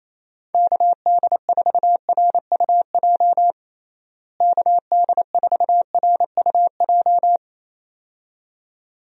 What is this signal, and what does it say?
Morse code, heard as a keyed tone, about 28 wpm, 700 Hz. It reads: KD4RUJ KD4RUJ
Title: USA Call Signs - 6 Characters - Repeated 2x - 28wpm